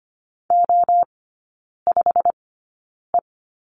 Morse code, heard as a keyed tone, about 25 wpm, spaced as for 9 wpm, 700 Hz. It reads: O5E
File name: Lesson 15e - Number 5 - Three Characters 25wpm